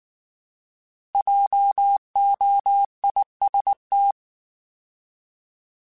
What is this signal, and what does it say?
Morse code, heard as a keyed tone, about 19 wpm, 800 Hz. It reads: JOIST
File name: Alphabetical / Alphabetical mp3s / Js @18